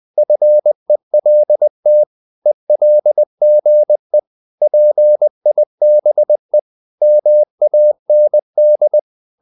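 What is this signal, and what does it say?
Morse code, heard as a keyed tone, about 20 wpm, 600 Hz. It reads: FELT ELGE PIBE MAND
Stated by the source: Morse code practice of short Danish words at 20 wpm, from CW med Gnister.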